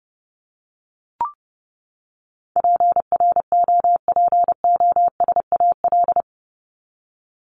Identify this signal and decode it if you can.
Morse code, heard as a keyed tone, about 30 wpm, 700 Hz. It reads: PROPOSAL
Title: Top 1751-2000 Words 30wpm